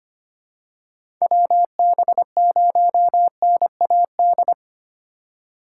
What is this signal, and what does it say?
Morse code, heard as a keyed tone, about 25 wpm, 700 Hz. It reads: WB0NAD
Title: USA Call Signs - 6 Characters - 3x Character Spacing - 25wpm